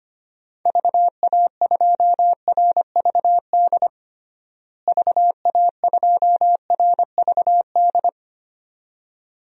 Morse code, heard as a keyed tone, about 25 wpm, 700 Hz. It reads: VA2RVD VA2RVD
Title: DXCC-100 Call Signs - 6 Characters - Repeated 2x - 25wpm